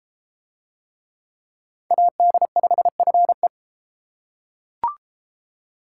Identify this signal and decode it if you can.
Morse code, heard as a keyed tone, about 33 wpm, 700 Hz. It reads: AD5FE